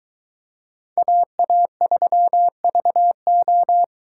Morse code, heard as a keyed tone, about 23 wpm, 700 Hz. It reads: AA3VO